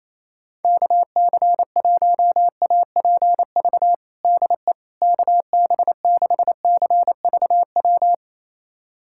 Morse code, heard as a keyed tone, about 28 wpm, 700 Hz. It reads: KC1APV DE KB6CVW
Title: USA Callsign DE Callsign 28wpm